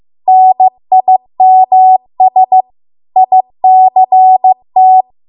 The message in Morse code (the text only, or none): NIMS ICT